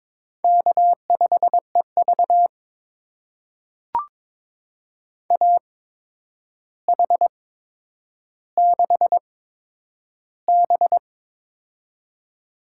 Morse code, heard as a keyed tone, about 22 wpm, spaced as for 9 wpm, 700 Hz. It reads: K5EV AH6B
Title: USA Call Signs - 4 Characters - 8x Character Spacing - 22wpm